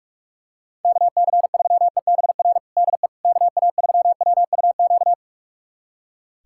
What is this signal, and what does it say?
Morse code, heard as a keyed tone, about 46 wpm, 700 Hz. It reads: KC3EBR DE KA3WUQ
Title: USA Callsign DE Callsign - 6 Characters - 45wpm